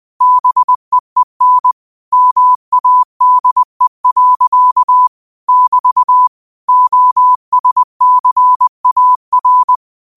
Morse code, heard as a keyed tone, about 20 wpm, 1000 Hz. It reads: BEEN MADE. <BT> OSCAR